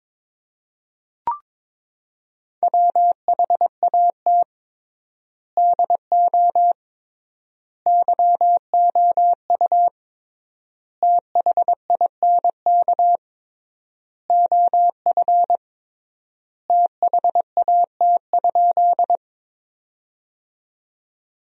Morse code, heard as a keyed tone, about 22 wpm, 700 Hz. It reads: WHAT DO YOU THINK OF THAT?
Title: 3x Word-Spacing - Sentences from Top 200 Words - 22wpm